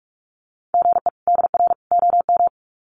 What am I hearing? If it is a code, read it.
Morse code, heard as a keyed tone, about 44 wpm, 700 Hz. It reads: GE DR OM
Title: Common QSO Phrases 45wpm